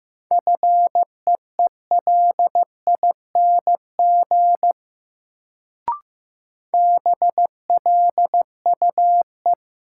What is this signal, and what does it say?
Morse code, heard as a keyed tone, about 15 wpm, 700 Hz. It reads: FEELING BLUE
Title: Sentences from Top 1000 Words - Review Part 3 - New Word Drill - 15wpm